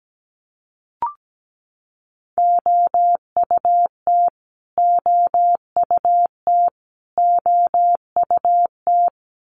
Morse code, heard as a keyed tone, about 17 wpm, 700 Hz. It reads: OUT OUT OUT
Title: Top 200 Words Repeated 17wpm